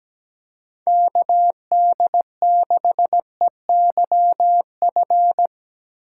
Morse code, heard as a keyed tone, about 17 wpm, 700 Hz. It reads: KD6EYF
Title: USA Call Signs - 6 Characters - 2x Character Spacing - 17wpm